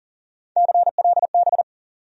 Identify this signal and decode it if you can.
Morse code, heard as a keyed tone, about 38 wpm, 700 Hz. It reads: CLB
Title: USA Call Sign Postfixes - 3 Letters 40wpm